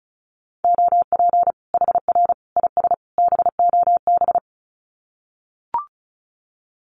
Morse code, heard as a keyed tone, about 34 wpm, 700 Hz. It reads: OP HR IS BOB
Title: Common QSO Phrases 35wpm